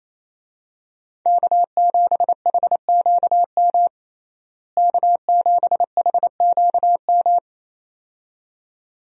Morse code, heard as a keyed tone, about 28 wpm, 700 Hz. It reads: K7HQM K7HQM
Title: USA Call Signs - 5 Characters - Repeated 2x - 28wpm